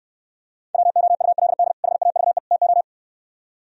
Morse code, heard as a keyed tone, about 68 wpm, 700 Hz. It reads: WOULD HAVE TO